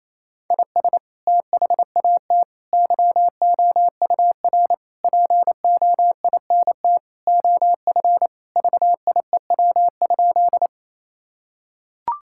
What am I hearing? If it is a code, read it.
Morse code, heard as a keyed tone, about 27 wpm, 700 Hz. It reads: IS THAT YOUR POINT OF VIEW?
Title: Sentences from Top 1000 Words - Review Part 6 - 28wpm